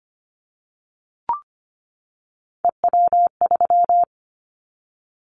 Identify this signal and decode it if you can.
Morse code, heard as a keyed tone, about 25 wpm, 700 Hz. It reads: EW3
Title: Callsign Prefixes for Top 50 DXCC - 25wpm